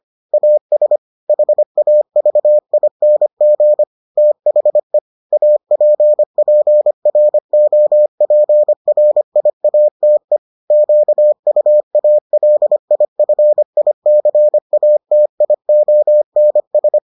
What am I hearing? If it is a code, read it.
Morse code, heard as a keyed tone, about 25 wpm, 600 Hz. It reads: AS HAVING THE APPROPRIATE QUALIFICATIONS